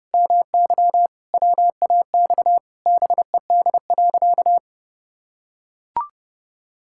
Morse code, heard as a keyed tone, about 32 wpm, 700 Hz. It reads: MY WAX BED.
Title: Pangram Challenge v2 - 30wpm